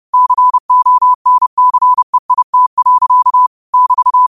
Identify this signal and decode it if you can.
Morse code, heard as a keyed tone, about 30 wpm, 1000 Hz. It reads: CONCEIT. <BT>
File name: qotdfiles / 30wpm